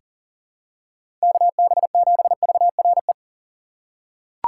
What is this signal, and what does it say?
Morse code, heard as a keyed tone, about 40 wpm, 700 Hz. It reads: KB7VRE
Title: USA Call Signs - 6 Characters - 4x Character Spacing - 40wpm